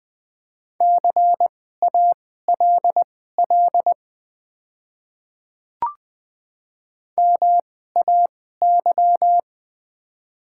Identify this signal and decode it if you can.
Morse code, heard as a keyed tone, about 20 wpm, spaced as for 14 wpm, 700 Hz. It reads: CALL MAY
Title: 2x Character Spacing - Top 100 Words - 20wpm